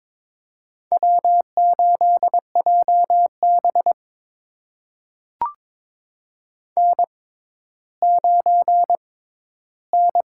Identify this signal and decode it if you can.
Morse code, heard as a keyed tone, about 22 wpm, spaced as for 13 wpm, 700 Hz. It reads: W8JB N9N
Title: USA Call Signs - 4 Characters - 6x Character Spacing - 22wpm